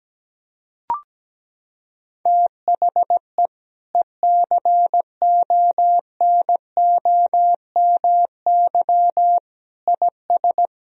Morse code, heard as a keyed tone, about 17 wpm, 700 Hz. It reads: THE ECONOMY IS
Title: Sentences from Top 1000 Words - Review Part 3 - 17wpm